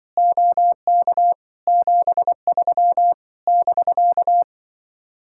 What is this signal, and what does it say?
Morse code, heard as a keyed tone, about 24 wpm, 700 Hz. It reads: OK 73 <BK>
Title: Speed-Racing Common QSO Phrases Encore - 20wpm